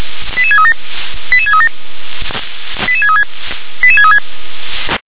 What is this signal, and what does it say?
Five-tone selective call (ZVEI), heard as quick runs of five tones, tones of about 70 ms. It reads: 88637 88637 88636 88636
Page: ZVEI